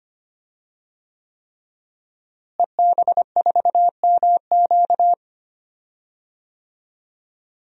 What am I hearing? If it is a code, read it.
Morse code, heard as a keyed tone, about 25 wpm, 700 Hz. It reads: EB4MQ
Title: DXCC-100 Call Signs - 5 Characters - Repeated 3x - 25wpm